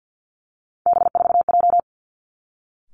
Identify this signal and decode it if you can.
Morse code, heard as a keyed tone, about 50 wpm, 700 Hz. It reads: B4J
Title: Three Challenging Letter-Number Combinations 50wpm